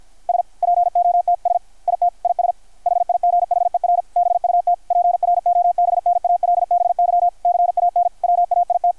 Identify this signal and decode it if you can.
Morse code, heard as a keyed tone, about 53 wpm, 720 Hz. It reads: SPOTS IT IS HIGHER BUT PROBNALBY CAN WAIT
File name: archived / CRSnet